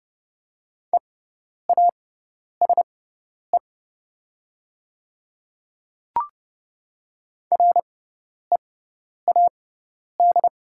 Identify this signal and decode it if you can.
Morse code, heard as a keyed tone, about 30 wpm, spaced as for 10 wpm, 690 Hz. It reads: EASE READ
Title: 6x Character Spacing - Top 500 Words - 30wpm